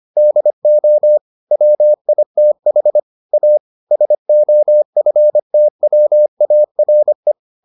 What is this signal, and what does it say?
Morse code, heard as a keyed tone, about 25 wpm, 600 Hz. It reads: DO WITH A SOFTWARE